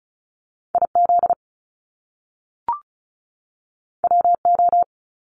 Morse code, heard as a keyed tone, about 36 wpm, 700 Hz. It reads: IZ WO
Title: Top Two Letter Combinations in Words 35wpm